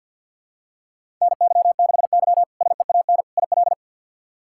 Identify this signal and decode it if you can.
Morse code, heard as a keyed tone, about 53 wpm, 700 Hz. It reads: NY6Y SEAN IL
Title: K1USN SST QSO Practice - 50wpm